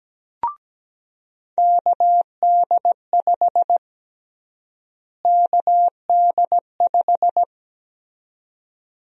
Morse code, heard as a keyed tone, about 17 wpm, 700 Hz. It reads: KD5 KD5
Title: USA Call Sign Prefixes - Repeated 2x - 17wpm